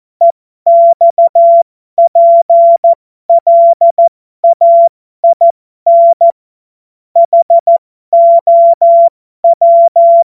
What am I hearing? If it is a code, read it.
Morse code, heard as a keyed tone, about 14 wpm, spaced as for 12 wpm, 680 Hz. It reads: EXPLAIN HOW